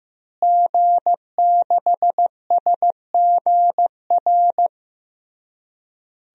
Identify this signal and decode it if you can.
Morse code, heard as a keyed tone, about 15 wpm, 700 Hz. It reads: G6SGR
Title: DXCC-100 Call Signs - 5 Characters - Repeated 3x - 15wpm